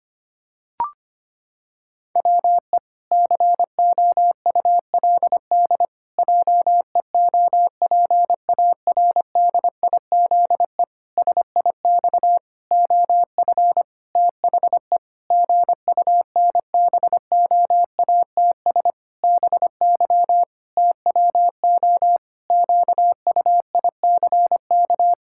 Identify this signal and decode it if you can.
Morse code, heard as a keyed tone, about 25 wpm, 700 Hz. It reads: WE COULD JEOPARDIZE SIX OF THE GUNBOATS BY TWO QUICK
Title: Pangram Challenge v2 - 25wpm